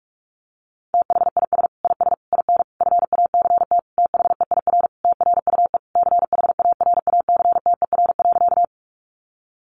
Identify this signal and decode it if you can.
Morse code, heard as a keyed tone, about 44 wpm, 700 Hz. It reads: THIS IS IN FACT THEIR TRUE CHARACTER.
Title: Sentences from Top 1200 Words 45wpm